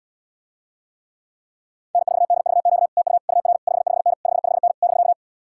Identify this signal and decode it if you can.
Morse code, heard as a keyed tone, about 75 wpm, 700 Hz. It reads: N4DVQ TU UR 55N 55N <BK>